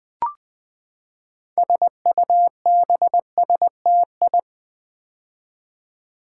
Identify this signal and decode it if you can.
Morse code, heard as a keyed tone, about 20 wpm, 700 Hz. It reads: SUBSTI
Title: Build A Word - 6 Letters At A Time - 12-Letter Words - 20wpm